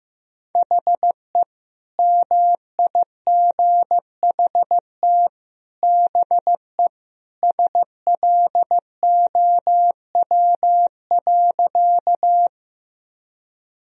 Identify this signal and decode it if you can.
Morse code, heard as a keyed tone, about 15 wpm, 700 Hz. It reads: HE MIGHT BE SLOW.
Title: Sentences from Top 1000 Words - Review Part 2 - 15wpm